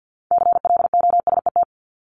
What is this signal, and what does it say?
Morse code, heard as a keyed tone, about 50 wpm, 700 Hz. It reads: CLOSET
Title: Top 1751-2000 Words 50wpm